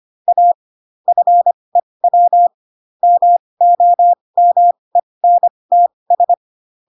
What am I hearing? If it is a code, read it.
Morse code, heard as a keyed tone, about 25 wpm, spaced as for 20 wpm, 700 Hz. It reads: A FEW MOMENTS
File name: Pinocchio / Pinocchio ch5 15 25 ik1hge